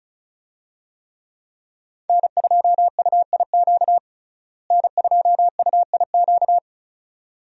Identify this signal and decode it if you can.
Morse code, heard as a keyed tone, about 36 wpm, 700 Hz. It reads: N2UIQ N2UIQ
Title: USA Call Signs - 5 Characters - Repeated 2x - 35wpm